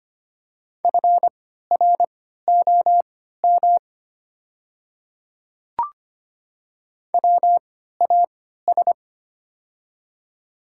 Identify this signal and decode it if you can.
Morse code, heard as a keyed tone, about 25 wpm, spaced as for 14 wpm, 700 Hz. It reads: FROM WAS
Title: 3x Character Spacing - Top 100 Words - 25wpm